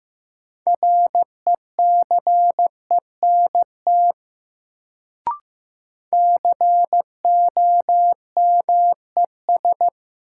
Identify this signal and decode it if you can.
Morse code, heard as a keyed tone, about 15 wpm, 700 Hz. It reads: RECENT COMES
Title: Sets of 3 Words - New Word Drill - 15wpm